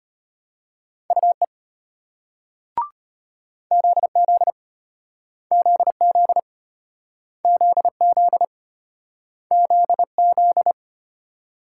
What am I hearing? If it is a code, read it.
Morse code, heard as a keyed tone, about 35 wpm, 700 Hz. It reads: UE ZZ ZZ ZZ ZZ